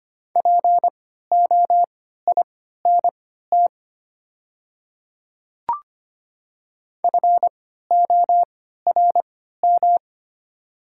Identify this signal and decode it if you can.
Morse code, heard as a keyed tone, about 25 wpm, spaced as for 14 wpm, 690 Hz. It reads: POINT FORM
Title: 3x Character Spacing - Top 200 Words - 25wpm